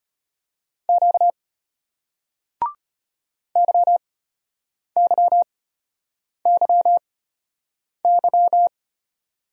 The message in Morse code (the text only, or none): Q YYYY